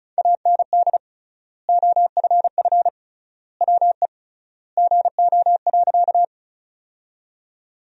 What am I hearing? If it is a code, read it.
Morse code, heard as a keyed tone, about 36 wpm, 700 Hz. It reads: AND OFF WE GO.
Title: 3x Word-Spacing - Sentences from Top 200 Words - 35wpm